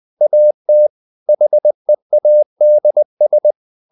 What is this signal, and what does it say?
Morse code, heard as a keyed tone, about 20 wpm, 600 Hz. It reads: AT HEADS
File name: the old man and the sea part 2 20wpm